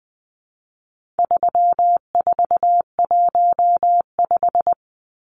Morse code, heard as a keyed tone, about 20 wpm, 700 Hz. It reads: 3415
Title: Numbers - 4-Digit 20wpm